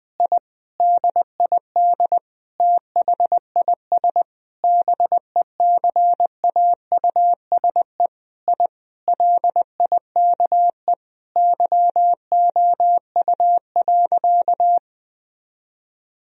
Morse code, heard as a keyed tone, about 20 wpm, 700 Hz. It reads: I DID THIS BECAUSE I LIKE YOU.
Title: Sentences from Top 1000 Words - Review Part 3 - 20wpm